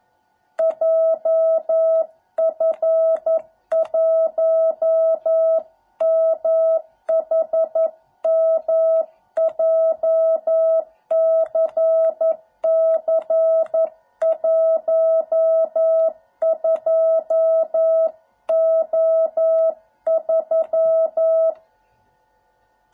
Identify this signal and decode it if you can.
Morse code, heard as a keyed tone, about 11 wpm, 660 Hz. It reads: JF1MHM JCC12O3